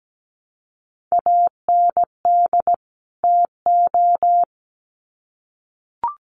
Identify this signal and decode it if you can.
Morse code, heard as a keyed tone, about 17 wpm, 700 Hz. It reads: AND TO
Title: Sets of 2 Words 17wpm